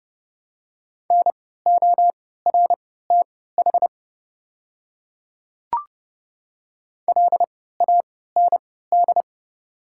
Morse code, heard as a keyed tone, about 31 wpm, spaced as for 17 wpm, 700 Hz. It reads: NORTH LAND
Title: 3x Character Spacing - Top 500 Words - 30wpm